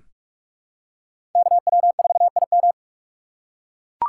Morse code, heard as a keyed tone, about 44 wpm, 690 Hz. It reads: KW4IM
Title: USA Call Signs - 5 Characters - 2x Character Spacing - 45wpm